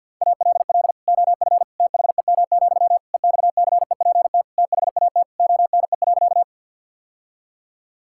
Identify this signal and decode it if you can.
Morse code, heard as a keyed tone, about 50 wpm, 700 Hz. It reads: ALL OF THEM, EXCEPT THAT ONE.
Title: Sentences from Top 1600 Words 50wpm